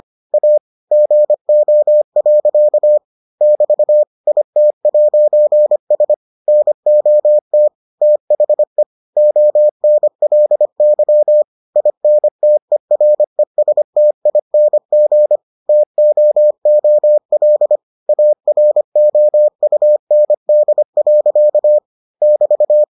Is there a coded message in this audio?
Yes: Morse code from 0.3 to 22.8 s, heard as a keyed tone, about 25 wpm, 600 Hz. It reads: A GO. <BT> IT'S NOT THE ONLY INTERESTING TOOL AROUND. <BT>